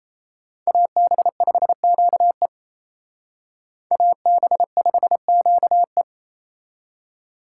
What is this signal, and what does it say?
Morse code, heard as a keyed tone, about 35 wpm, 700 Hz. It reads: AB5QE AB5QE